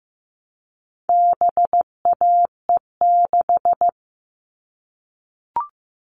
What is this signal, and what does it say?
Morse code, heard as a keyed tone, about 15 wpm, 700 Hz. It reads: BAE6
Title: All Cut Numbers - 4-Digit 15wpm